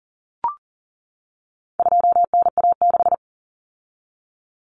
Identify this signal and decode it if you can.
Morse code, heard as a keyed tone, about 40 wpm, 700 Hz. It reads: 2NA6
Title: Common Cut Numbers - 4-Digit 40wpm